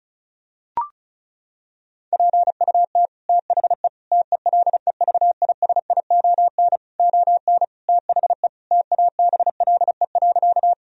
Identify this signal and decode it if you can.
Morse code, heard as a keyed tone, about 36 wpm, 700 Hz. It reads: PUT THE TELEVISION ON THE TABLE.
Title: Sentences from Top 1000 Words - Review Part 4 - 35wpm